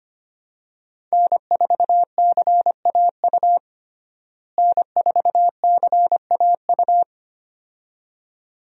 Morse code, heard as a keyed tone, about 25 wpm, 700 Hz. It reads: N4CAU N4CAU
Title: USA Call Signs - 5 Characters - Repeated 2x - 25wpm